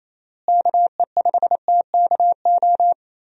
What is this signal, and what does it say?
Morse code, heard as a keyed tone, about 28 wpm, 700 Hz. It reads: KE5TKO
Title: USA Call Signs - 6 Characters - Repeated 2x - 28wpm